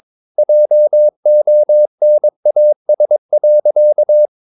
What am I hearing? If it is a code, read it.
Morse code, heard as a keyed tone, about 22 wpm, 600 Hz. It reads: JONAS.
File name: Ferie QRS 22wpm